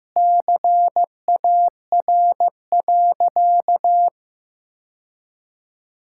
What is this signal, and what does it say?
Morse code, heard as a keyed tone, about 15 wpm, 700 Hz. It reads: CAR.
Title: Sentences from Top 1000 Words - Review Part 5 - 15wpm